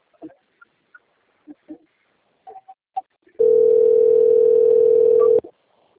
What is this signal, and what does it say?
Ringback tone (North America), 440 + 480 Hz.